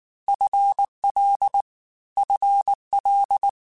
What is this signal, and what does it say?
Morse code, heard as a keyed tone, about 19 wpm, 800 Hz. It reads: FL FL
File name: FL / FL mp3s / FL Prefixes 3x2 @18